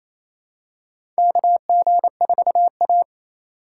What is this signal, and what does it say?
Morse code, heard as a keyed tone, about 28 wpm, 700 Hz. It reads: KG4A
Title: USA Call Signs - 4 Characters - 2x Character Spacing - 28wpm